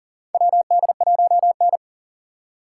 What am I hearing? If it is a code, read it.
Morse code, heard as a keyed tone, about 40 wpm, 700 Hz. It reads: WD1N
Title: USA Call Signs - 2x Character Spacing - 40wpm